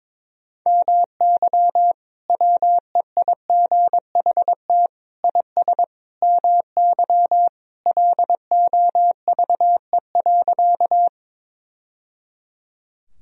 Morse code, heard as a keyed tone, about 22 wpm, 700 Hz. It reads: MY WEIGHT IS MY LOVE.
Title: Sentences from Top 1000 Words - Review Part 3 - 22wpm